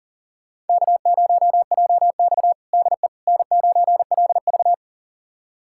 Morse code, heard as a keyed tone, about 40 wpm, 700 Hz. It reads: K0JX DE N9LV
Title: USA Callsign DE Callsign - 4 Characters - 40wpm